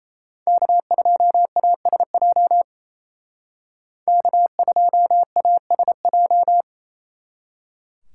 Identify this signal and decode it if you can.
Morse code, heard as a keyed tone, about 33 wpm, 700 Hz. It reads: K2ASJ K2ASJ
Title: Speed Racing - USA 1x3 Call Signs 28wpm